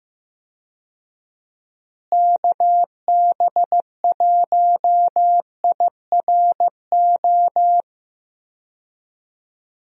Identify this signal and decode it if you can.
Morse code, heard as a keyed tone, about 15 wpm, 700 Hz. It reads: KB1IRO
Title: USA Call Signs - 6 Characters - Repeated 3x - 15wpm